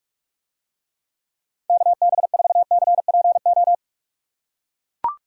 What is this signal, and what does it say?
Morse code, heard as a keyed tone, about 44 wpm, 700 Hz. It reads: KB4CPO